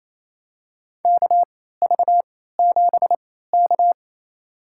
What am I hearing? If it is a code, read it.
Morse code, heard as a keyed tone, about 28 wpm, spaced as for 16 wpm, 700 Hz. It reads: KV7K
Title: USA Call Signs - 4 Characters - 3x Character Spacing - 28wpm